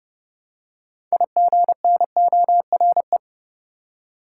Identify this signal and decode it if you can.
Morse code, heard as a keyed tone, about 31 wpm, 700 Hz. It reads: IGNORE
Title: Six Letter Words - 30wpm